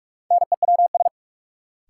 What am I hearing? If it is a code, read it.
Morse code, heard as a keyed tone, about 46 wpm, 700 Hz. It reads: NEWS
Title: Sentences from Top 800 Words - New Word Drill - 45wpm